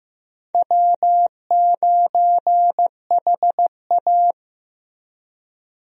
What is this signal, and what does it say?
Morse code, heard as a keyed tone, about 16 wpm, 700 Hz. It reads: W9HA
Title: USA Call Signs - 4 Characters - Repeated 3x - 15wpm